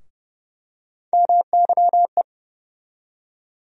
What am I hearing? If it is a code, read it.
Morse code, heard as a keyed tone, about 30 wpm, 700 Hz. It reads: MYE
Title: USA Call Sign Postfixes - 3 Letters 30wpm